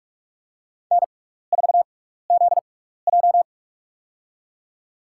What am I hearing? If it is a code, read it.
Morse code, heard as a keyed tone, about 44 wpm, spaced as for 15 wpm, 700 Hz. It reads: N4ZJ
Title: USA Call Signs - 4 Characters - 6x Character Spacing - 45wpm